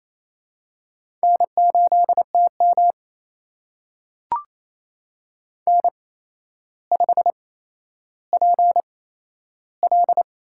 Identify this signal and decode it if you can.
Morse code, heard as a keyed tone, about 28 wpm, spaced as for 12 wpm, 700 Hz. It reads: N8TM N5PL
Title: USA Call Signs - 8x Character Spacing - 28wpm